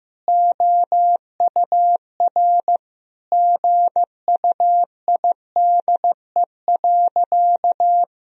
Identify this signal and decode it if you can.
Morse code, heard as a keyed tone, about 15 wpm, 700 Hz. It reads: OUR GUIDE.
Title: Sentences from Top 1000 Words - Review Part 3 - 15wpm